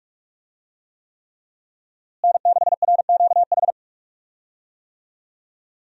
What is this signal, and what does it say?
Morse code, heard as a keyed tone, about 42 wpm, 700 Hz. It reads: N6RQH
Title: USA Call Signs - 5 Characters - Repeated 3x - 45wpm